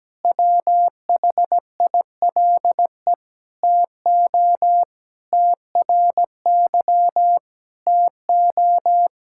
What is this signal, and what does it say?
Morse code, heard as a keyed tone, about 17 wpm, 700 Hz. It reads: WHILE TO TRY TO
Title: Sentences from Top 1600 Words 17wpm